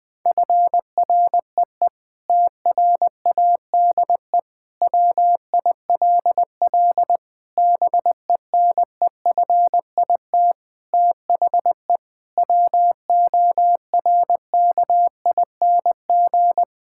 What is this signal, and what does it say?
Morse code, heard as a keyed tone, about 21 wpm, 700 Hz. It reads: FREE TRADE WILL BENEFIT THE WORKING
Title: Sentences from Top 1000 Words - Review Part 6 - 20wpm